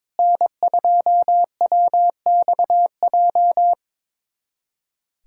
Morse code, heard as a keyed tone, about 22 wpm, 700 Hz. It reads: N2WXJ
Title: USA Callsign DE Callsign - 5 Characters - 22wpm